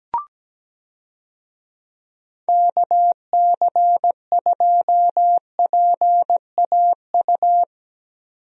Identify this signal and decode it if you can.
Morse code, heard as a keyed tone, about 17 wpm, 700 Hz. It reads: KC2PAU
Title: USA Call Signs - 6 Characters - 17wpm